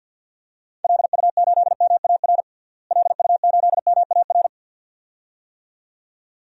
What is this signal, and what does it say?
Morse code, heard as a keyed tone, about 48 wpm, 700 Hz. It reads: LU8MAR LU8MAR
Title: DXCC-100 Call Signs - 6 Characters - Repeated 2x - 50wpm